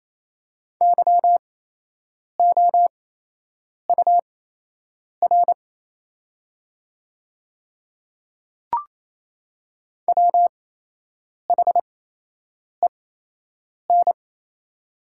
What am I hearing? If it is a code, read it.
Morse code, heard as a keyed tone, about 27 wpm, spaced as for 8 wpm, 700 Hz. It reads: YOUR WHEN
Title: 8x Character Spacing - Top 100 Words - 28wpm